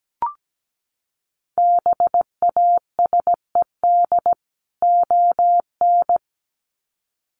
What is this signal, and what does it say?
Morse code, heard as a keyed tone, about 17 wpm, 700 Hz. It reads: BASED ON
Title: Sets of 2 Words 17wpm